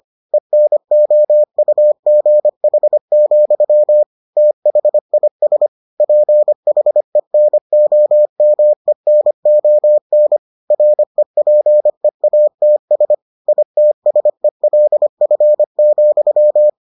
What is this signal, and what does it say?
Morse code, heard as a keyed tone, about 25 wpm, 600 Hz. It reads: ENOUGH, THIS PHENOMENON REPEATS ITSELF,